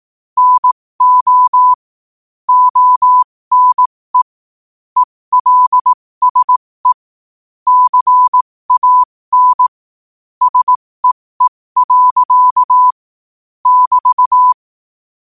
Morse code, heard as a keyed tone, about 18 wpm, spaced as for 15 wpm, 1000 Hz. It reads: NO ONE ELSE CAN SEE. <BT>
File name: qotdfiles / 15wpm